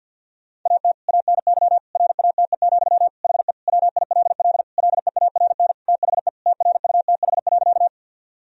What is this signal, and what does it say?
Morse code, heard as a keyed tone, about 50 wpm, 700 Hz. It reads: AT ANY RATE, HE WILL LEARN THE TRUTH.
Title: Sentences from Top 1000 Words - Review Part 4 - 50wpm